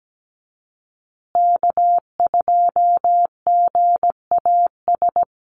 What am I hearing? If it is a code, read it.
Morse code, heard as a keyed tone, about 17 wpm, 700 Hz. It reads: K2GAS
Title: USA Call Signs Repeated 2x 17wpm